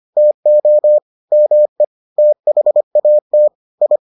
Morse code, heard as a keyed tone, about 25 wpm, 600 Hz. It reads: TO ME THAT I